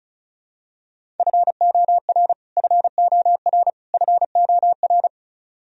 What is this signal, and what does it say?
Morse code, heard as a keyed tone, about 34 wpm, 700 Hz. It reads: FOR FOR FOR
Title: Top 100 Words Repeated 35wpm